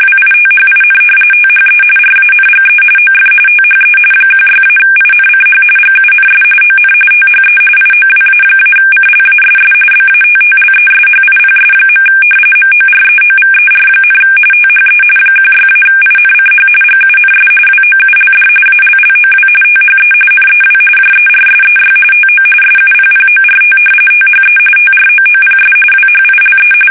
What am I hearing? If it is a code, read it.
RTTY, about 45.45 baud. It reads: WITHIN THE RANGE OF THE 16-INCH GUNS ON THE IOWA-CLASS BATTLESHIPS. MOREOVER, THE 16-INCH GUNS WOULD ALLOW THE NEW JERSEY TO SUPPORT THE TROOPS WHILE
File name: fsk850 / battleshipnj